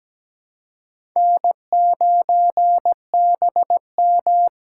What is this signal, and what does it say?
Morse code, heard as a keyed tone, about 17 wpm, 700 Hz. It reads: N9BM
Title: USA Call Signs - 4x Character Spacing - 17wpm